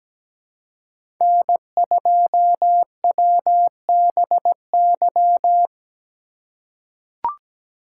Morse code, heard as a keyed tone, about 17 wpm, 700 Hz. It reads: N2WBY
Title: USA Call Signs - 5 Characters - 3x Character Spacing - 17wpm